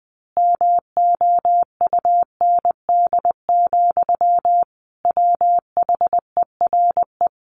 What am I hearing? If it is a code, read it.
Morse code, heard as a keyed tone, about 20 wpm, 700 Hz. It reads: MOUND, WHERE
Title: A Sound Of Thunder 20wpm